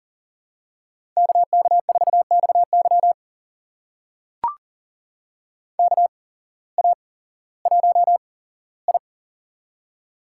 Morse code, heard as a keyed tone, about 38 wpm, spaced as for 18 wpm, 700 Hz. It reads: KK4XY KA1I
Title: USA Call Signs - 8x Character Spacing - 40wpm